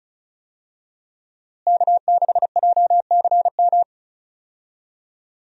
Morse code, heard as a keyed tone, about 36 wpm, 700 Hz. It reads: K6JCM
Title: USA Call Signs - 5 Characters - Repeated 2x - 35wpm